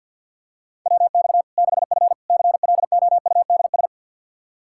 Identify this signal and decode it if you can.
Morse code, heard as a keyed tone, about 51 wpm, 700 Hz. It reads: WX 6F CLOUDS
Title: Practice QSO - Exchange 2 - Info - 50wpm